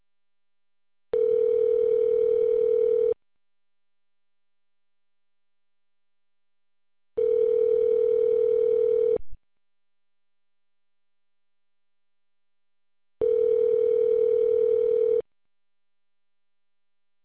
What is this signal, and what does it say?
Ringback tone (North America), 440 + 480 Hz.